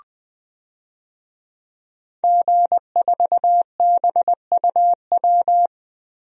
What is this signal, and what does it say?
Morse code, heard as a keyed tone, about 20 wpm, 700 Hz. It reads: G4BUW
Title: DXCC-100 Call Signs - 5 Characters - Repeated 3x - 20wpm